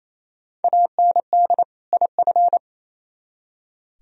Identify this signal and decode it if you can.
Morse code, heard as a keyed tone, about 34 wpm, 690 Hz. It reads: AND IF
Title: Speed-Racing - Sets of 2 Words 28wpm